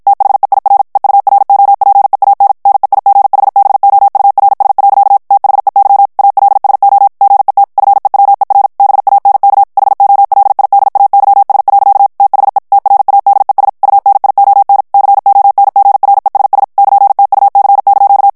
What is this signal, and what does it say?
Morse code, heard as a keyed tone, about 53 wpm, 800 Hz. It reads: THEIR EUROPEAN NEIGHBOURS. THEY ALSO GET FEWER BANK HOLIDAYS. THE TRADES UNION CONGRESS <KN>TUC)
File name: morse mp3 / ANGLAIS texte clair 1 vitesse 50 mots minute